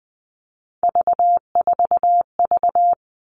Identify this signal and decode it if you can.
Morse code, heard as a keyed tone, about 20 wpm, 700 Hz. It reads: V4V
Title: All Cut Numbers - 3-Digit 20wpm